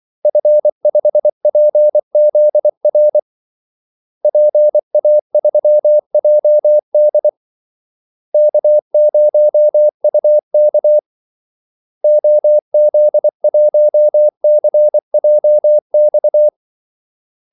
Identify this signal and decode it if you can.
Morse code, heard as a keyed tone, about 24 wpm, 600 Hz. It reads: F5PZR PA3JD K0UK OZ1CJX
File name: Callsigns-24wpm